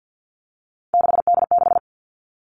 Morse code, heard as a keyed tone, about 50 wpm, 700 Hz. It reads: 6D6
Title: All Cut Numbers - 3-Digit 50wpm